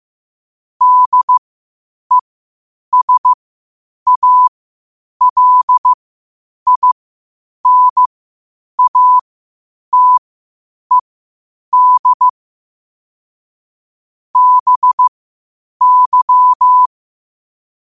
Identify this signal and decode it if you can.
Morse code, heard as a keyed tone, about 15 wpm, spaced as for 8 wpm, 1000 Hz. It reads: DESALINATED BY
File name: qotdfiles / 8wpm